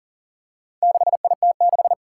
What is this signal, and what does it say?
Morse code, heard as a keyed tone, about 40 wpm, 700 Hz. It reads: 6IT6
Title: Four Challenging Letter-Number Combinations 40wpm